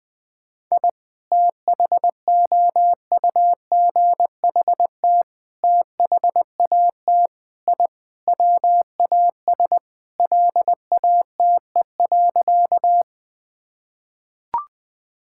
Morse code, heard as a keyed tone, about 20 wpm, 700 Hz. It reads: I THOUGHT THAT I WAS LATE.
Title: Sentences with 3-Word Phrases - 20wpm